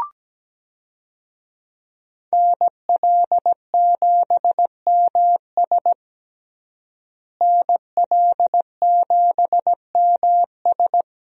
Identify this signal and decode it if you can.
Morse code, heard as a keyed tone, about 17 wpm, 700 Hz. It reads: NL7MS NL7MS
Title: USA Call Signs - 5 Characters - Repeated 3x - 17wpm